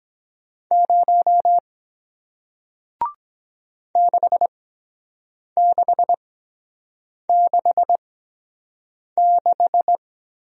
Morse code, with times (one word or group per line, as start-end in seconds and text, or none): 0.7-1.6 0
4.0-10.0 6666